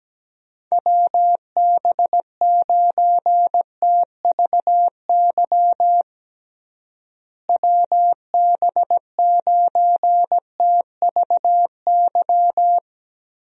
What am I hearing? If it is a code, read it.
Morse code, heard as a keyed tone, about 17 wpm, 700 Hz. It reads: WB9TVY WB9TVY
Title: USA Call Signs - 6 Characters - Repeated 2x - 17wpm